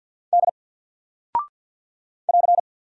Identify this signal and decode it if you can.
Morse code, heard as a keyed tone, about 48 wpm, 700 Hz. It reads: D <AR>
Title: Single Character (Letter:Number:Punctuation:Prosign) Alt 50wpm